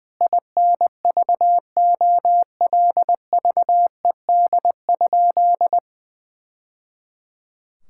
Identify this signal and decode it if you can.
Morse code, heard as a keyed tone, about 20 wpm, 700 Hz. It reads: INVOLVED?
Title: Sentences from Top 1000 Words - Review Part 5 - 20wpm